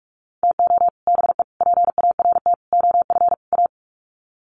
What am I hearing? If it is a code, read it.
Morse code, heard as a keyed tone, about 48 wpm, 700 Hz. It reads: TO BE PART OF A
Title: Sets of 5 Words Encore2 45wpm